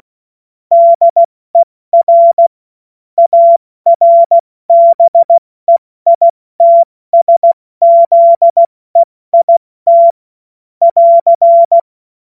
Morse code, heard as a keyed tone, about 16 wpm, 680 Hz. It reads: DER ARBEITSZEIT <AR>
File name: Morsekurs / Witze / Witz 24.Oktober 2023 70 F80